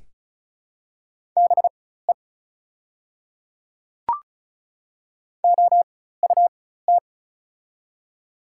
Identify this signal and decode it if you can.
Morse code, heard as a keyed tone, about 36 wpm, spaced as for 16 wpm, 700 Hz. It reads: BE OUT
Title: 4x Character Spacing - Top 100 Words - 35wpm